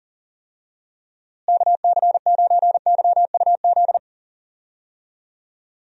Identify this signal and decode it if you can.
Morse code, heard as a keyed tone, about 37 wpm, 700 Hz. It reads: KC9YUZ
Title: USA Call Signs - 6 Characters - Repeated 2x - 40wpm